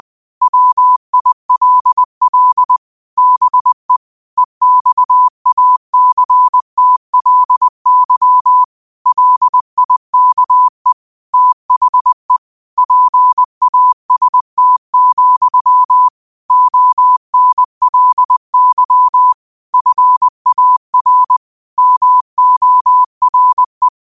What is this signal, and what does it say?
Morse code, heard as a keyed tone, about 20 wpm, 1000 Hz. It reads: WILL BE EXACTLY LIKE THE PAST, ONLY FAR MORE